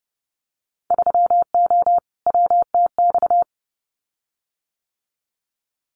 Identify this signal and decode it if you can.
Morse code, heard as a keyed tone, about 30 wpm, 700 Hz. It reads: 3O WTX
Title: Winter Field Day Exchanges Encore 30wpm